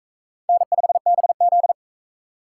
Morse code, heard as a keyed tone, about 43 wpm, 700 Hz. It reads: N5BZ